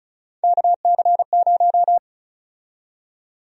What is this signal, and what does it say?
Morse code, heard as a keyed tone, about 34 wpm, 700 Hz. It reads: KC0
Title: USA Call Sign Prefixes - 35wpm